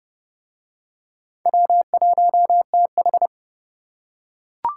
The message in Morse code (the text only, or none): W1TH